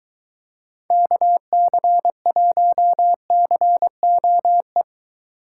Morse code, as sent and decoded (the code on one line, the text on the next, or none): -.- -.-. .---- -.-. --- .
KC1COE